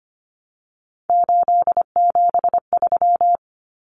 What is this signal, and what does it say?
Morse code, heard as a keyed tone, about 25 wpm, 700 Hz. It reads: 873
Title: Numbers - 3-Digit 25wpm